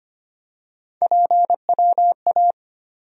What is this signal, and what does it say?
Morse code, heard as a keyed tone, about 25 wpm, 700 Hz. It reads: PWA
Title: USA Call Sign Postfixes - 3 Letters 25wpm